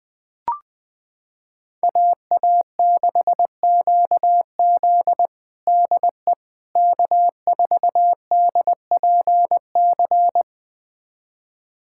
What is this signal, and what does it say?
Morse code, heard as a keyed tone, about 20 wpm, 700 Hz. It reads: AA6QZ DE K4DPC
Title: USA Callsign DE Callsign - 5 Characters - 20wpm